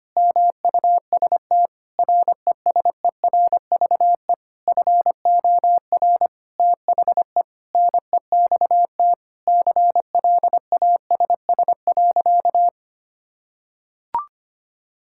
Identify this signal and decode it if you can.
Morse code, heard as a keyed tone, about 25 wpm, 700 Hz. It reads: MUST RESERVE FOR THE NEXT CLASS.
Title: Sentences from Top 1000 Words - Review Part 6 - 25wpm